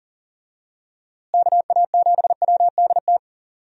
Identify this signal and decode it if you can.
Morse code, heard as a keyed tone, about 40 wpm, 700 Hz. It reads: KA7WDT
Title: USA Call Signs - 6 Characters - Repeated 3x - 40wpm